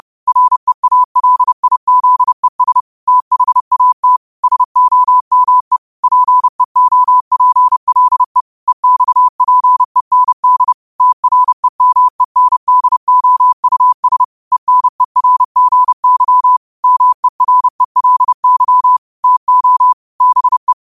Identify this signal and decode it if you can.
Morse code, heard as a keyed tone, about 30 wpm, 1000 Hz. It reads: REALIZES THAT SOME PEOPLE EXPEND TREMENDOUS ENERGY MERELY TO BE
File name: qotdfiles / 30wpm